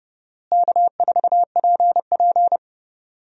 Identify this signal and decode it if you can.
Morse code, heard as a keyed tone, about 29 wpm, 700 Hz. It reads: K4PP
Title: USA Call Signs - 4 Characters - Repeated 3x - 30wpm